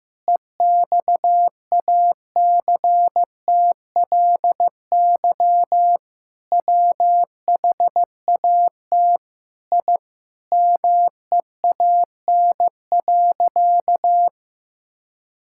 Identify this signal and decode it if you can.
Morse code, heard as a keyed tone, about 15 wpm, 700 Hz. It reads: EXACTLY WHAT I MEAN.
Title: Sentences from Top 1000 Words - Review Part 6 - 15wpm